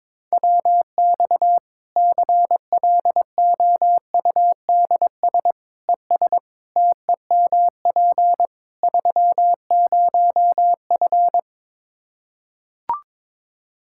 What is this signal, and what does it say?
Morse code, heard as a keyed tone, about 22 wpm, 700 Hz. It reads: WX CLOUDS ES TEMP 30F
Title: K1USN SST QSO Practice 22wpm